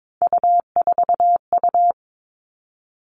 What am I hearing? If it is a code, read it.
Morse code, heard as a keyed tone, about 22 wpm, 700 Hz. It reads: U4U
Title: Three Challenging Letter-Number Combinations 22wpm